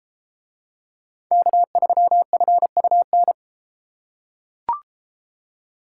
Morse code, heard as a keyed tone, about 31 wpm, 700 Hz. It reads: K3FUN